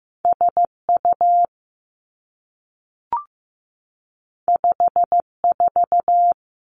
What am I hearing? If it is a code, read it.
Morse code, heard as a keyed tone, about 15 wpm, 700 Hz. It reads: S U 5 4
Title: Two Challenging Letter-Number Combinations 15wpm